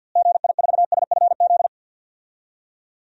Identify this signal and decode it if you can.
Morse code, heard as a keyed tone, about 53 wpm, 700 Hz. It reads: GI4SFZ
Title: DXCC-100 Call Signs - 6 Characters - Repeated 2x - 50wpm